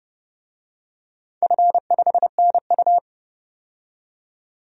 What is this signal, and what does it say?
Morse code, heard as a keyed tone, about 29 wpm, 700 Hz. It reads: F5NU
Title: DXCC-100 Call Signs - 4 Characters - 30wpm